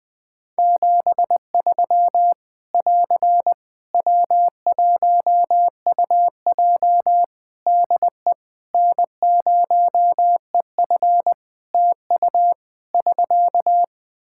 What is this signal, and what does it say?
Morse code, heard as a keyed tone, about 20 wpm, 700 Hz. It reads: 73 <AR> W1UJ DE N0EF TU <SK>
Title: Practice QSO - Exchange 4 - Ending - 20wpm